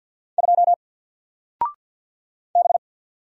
Morse code, heard as a keyed tone, about 48 wpm, 700 Hz. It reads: 2 B
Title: Single Letter-Number Alt 50wpm